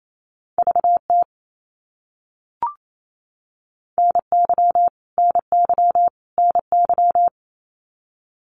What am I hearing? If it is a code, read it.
Morse code, heard as a keyed tone, about 28 wpm, 700 Hz. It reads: VT NY NY NY
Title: US States Repeated 3x 28wpm